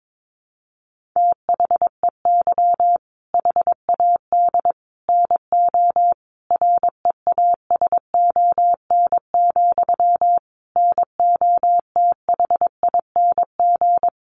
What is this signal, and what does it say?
Morse code, heard as a keyed tone, about 22 wpm, 700 Hz. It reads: THEY HAD NO REASON, NOTHING
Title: Sentences from Top 1500 Words 22wpm